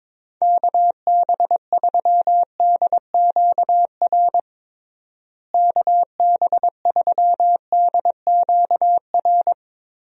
Morse code, heard as a keyed tone, about 22 wpm, 700 Hz. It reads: KB3DQR KB3DQR
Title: USA Call Signs - 6 Characters - Repeated 3x - 22wpm